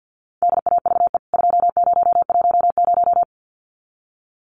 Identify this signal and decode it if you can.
Morse code, heard as a keyed tone, about 50 wpm, 700 Hz. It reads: DAVE 2010
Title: CWT Exchange 50wpm